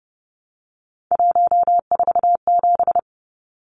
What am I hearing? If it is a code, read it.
Morse code, heard as a keyed tone, about 30 wpm, 700 Hz. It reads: 147
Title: Numbers - 3-Digit 30wpm